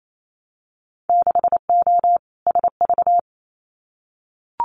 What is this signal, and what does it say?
Morse code, heard as a keyed tone, about 28 wpm, 700 Hz. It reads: 6O SV
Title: Winter Field Day Exchanges Encore 28wpm